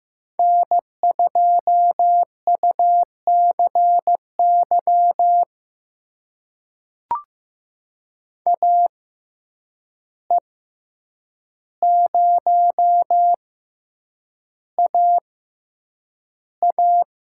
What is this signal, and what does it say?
Morse code, heard as a keyed tone, about 15 wpm, spaced as for 8 wpm, 700 Hz. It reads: N2UCY AE0AA
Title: USA Call Signs - 5 Characters - 6x Character Spacing - 15wpm